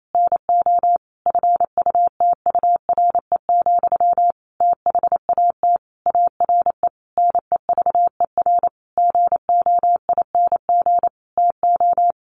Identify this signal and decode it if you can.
Morse code, heard as a keyed tone, about 28 wpm, 700 Hz. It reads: NO FUTURE, THAT ARE NEVER GOING TO
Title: A Sound Of Thunder 28wpm